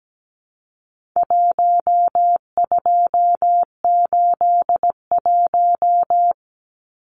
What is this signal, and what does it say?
Morse code, heard as a keyed tone, about 17 wpm, 700 Hz. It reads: 1281
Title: Numbers - 4-Digit 17wpm